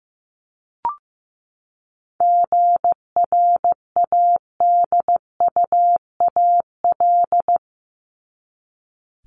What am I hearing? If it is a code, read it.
Morse code, heard as a keyed tone, about 15 wpm, 700 Hz. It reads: GRADUAL
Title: LY Suffix - Top 100 Words with Base Words 15wpm